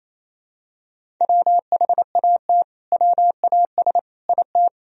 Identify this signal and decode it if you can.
Morse code, heard as a keyed tone, about 28 wpm, 700 Hz. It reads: WHAT WAS IT
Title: Sentences from Top 100 Words and Abbreviations 28wpm